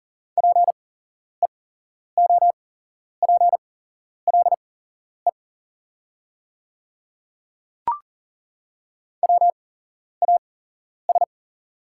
Morse code, heard as a keyed tone, about 40 wpm, spaced as for 11 wpm, 700 Hz. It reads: PEOPLE WAS
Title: 8x Character Spacing - Top 100 Words - 40wpm